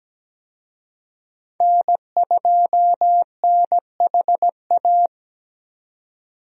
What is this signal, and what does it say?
Morse code, heard as a keyed tone, about 17 wpm, 700 Hz. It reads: N2NHA
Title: USA Call Signs - 5 Characters - 17wpm